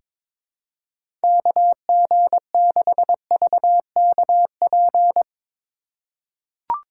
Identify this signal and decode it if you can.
Morse code, heard as a keyed tone, about 22 wpm, 700 Hz. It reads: KG6VKP
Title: USA Call Signs - 6 Characters - 2x Character Spacing - 22wpm